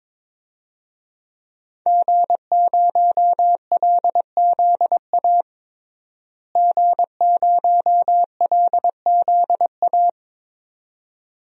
Morse code, heard as a keyed tone, about 22 wpm, 700 Hz. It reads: G0LZA G0LZA
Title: DXCC-100 Call Signs - 5 Characters - Repeated 2x - 22wpm